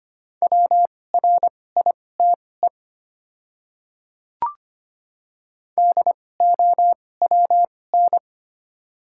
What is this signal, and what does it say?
Morse code, heard as a keyed tone, about 25 wpm, spaced as for 18 wpm, 690 Hz. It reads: WRITE DOWN
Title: 2x Character Spacing - Top 100 Words - 25wpm